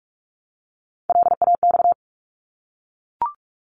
Morse code, heard as a keyed tone, about 44 wpm, 700 Hz. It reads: LAX
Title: ARRL Sections 45wpm